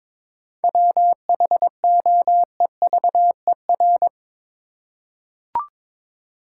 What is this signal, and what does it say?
Morse code, heard as a keyed tone, about 22 wpm, 700 Hz. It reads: WHOEVER
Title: Seven Letter Words - 22wpm